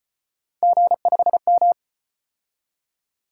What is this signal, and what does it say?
Morse code, heard as a keyed tone, about 34 wpm, 700 Hz. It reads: G5M